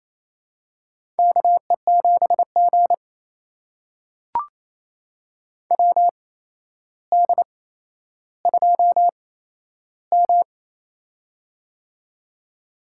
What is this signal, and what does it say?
Morse code, heard as a keyed tone, about 28 wpm, spaced as for 12 wpm, 700 Hz. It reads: KE7G WD2M
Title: USA Call Signs - 4 Characters - 8x Character Spacing - 28wpm